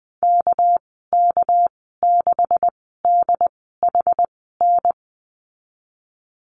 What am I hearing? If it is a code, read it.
Morse code, heard as a keyed tone, about 20 wpm, spaced as for 15 wpm, 700 Hz. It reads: KK6DHN
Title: 219 Lesson 35g - Number 0 - Call Signs 20wpm